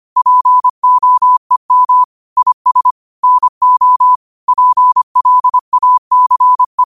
Morse code, heard as a keyed tone, about 25 wpm, 1000 Hz. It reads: POEM IS NO PLACE